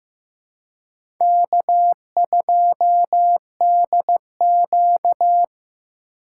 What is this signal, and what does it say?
Morse code, heard as a keyed tone, about 15 wpm, 700 Hz. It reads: K2DQ
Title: USA Call Signs - 2x Character Spacing - 15wpm